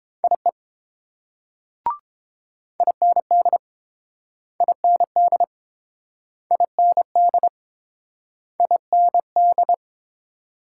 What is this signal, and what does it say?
Morse code, heard as a keyed tone, about 29 wpm, 700 Hz. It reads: IE IND IND IND IND